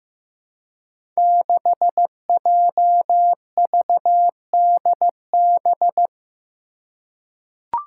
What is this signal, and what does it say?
Morse code, heard as a keyed tone, about 15 wpm, 700 Hz. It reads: 6JVDB
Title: Five Challenging Letter-Number Combinations 15wpm